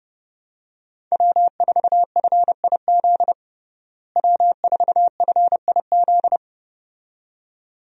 Morse code, heard as a keyed tone, about 30 wpm, 700 Hz. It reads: W4FIZ W4FIZ
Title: USA Call Signs - 5 Characters - Repeated 2x - 30wpm